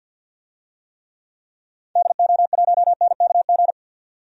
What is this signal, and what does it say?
Morse code, heard as a keyed tone, about 52 wpm, 700 Hz. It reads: DO1NKG
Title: DXCC-100 Call Signs - 6 Characters - 50wpm